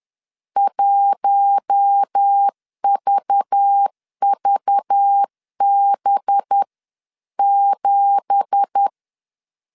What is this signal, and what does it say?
Morse code, heard as a keyed tone, about 10 wpm, 800 Hz. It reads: 1VVB7